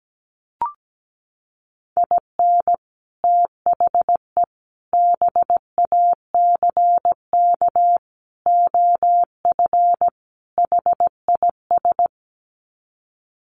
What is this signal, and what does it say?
Morse code, heard as a keyed tone, about 17 wpm, 700 Hz. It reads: IN THE BACK OF HIS
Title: Sets of 5 Words Encore2 17wpm